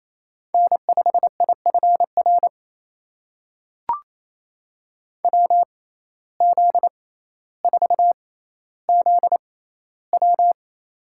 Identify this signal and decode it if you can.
Morse code, heard as a keyed tone, about 28 wpm, spaced as for 14 wpm, 700 Hz. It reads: N5IFR WZ4ZW
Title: USA Call Signs - 5 Characters - 6x Character Spacing - 28wpm